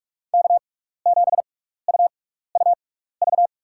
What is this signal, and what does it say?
Morse code, heard as a keyed tone, about 47 wpm, spaced as for 16 wpm, 700 Hz. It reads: K7UUV
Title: USA Call Signs - 6x Character Spacing - 45wpm